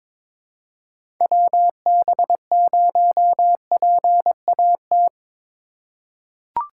Morse code, heard as a keyed tone, about 22 wpm, 700 Hz. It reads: WB0PAT
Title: USA Call Signs - 6 Characters - 3x Character Spacing - 22wpm